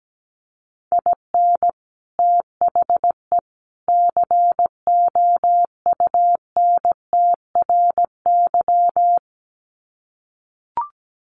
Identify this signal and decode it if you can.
Morse code, heard as a keyed tone, about 17 wpm, 700 Hz. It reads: IN THE COUNTRY
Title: Sets of 3 Words 17wpm